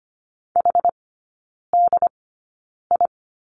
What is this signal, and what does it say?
Morse code, heard as a keyed tone, about 25 wpm, spaced as for 9 wpm, 700 Hz. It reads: HDI
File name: Lesson 14e - Number 2 - Three Characters 25wpm